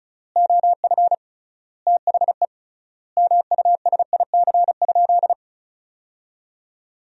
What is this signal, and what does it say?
Morse code, heard as a keyed tone, about 36 wpm, 700 Hz. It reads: OF THE MUSIC?
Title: 3x Word-Spacing - Sentences from Top 300 Words - 35wpm